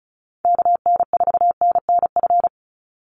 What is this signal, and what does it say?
Morse code, heard as a keyed tone, about 34 wpm, 700 Hz. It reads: KN4NNF
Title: USA Call Signs 35wpm